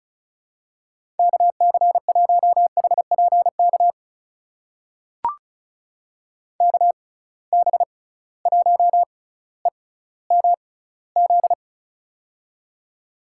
Morse code, heard as a keyed tone, about 36 wpm, spaced as for 18 wpm, 700 Hz. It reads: KC1HPK KB1EMZ
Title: USA Call Signs - 6 Characters - 6x Character Spacing - 35wpm